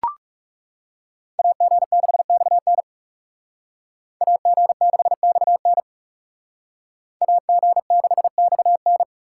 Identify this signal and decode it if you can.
Morse code, heard as a keyed tone, about 37 wpm, 700 Hz. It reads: AG6XN AG6XN AG6XN
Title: Speed Racing - USA 2x2 Call Signs 30wpm